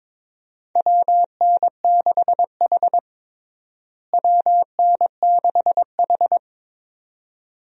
Morse code, heard as a keyed tone, about 22 wpm, 700 Hz. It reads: WN6H WN6H
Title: USA Call Signs - 4 Characters - Repeated 2x - 22wpm